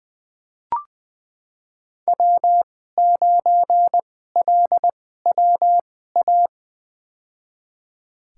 Morse code, heard as a keyed tone, about 20 wpm, spaced as for 15 wpm, 700 Hz. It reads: W9LWA
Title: USA Call Signs - 5 Characters - 2x Character Spacing - 20wpm